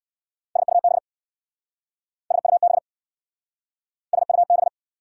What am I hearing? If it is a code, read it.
Morse code, heard as a keyed tone, about 63 wpm, 690 Hz. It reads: SUB SUB SUB